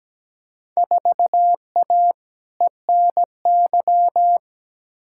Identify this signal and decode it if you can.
Morse code, heard as a keyed tone, about 17 wpm, 700 Hz. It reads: 4A ENY
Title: Field Day Exchanges - 17wpm